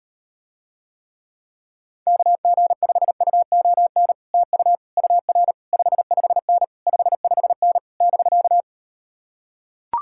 Morse code, heard as a keyed tone, about 38 wpm, 700 Hz. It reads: KG5UON TU UR 55N 55N <BK>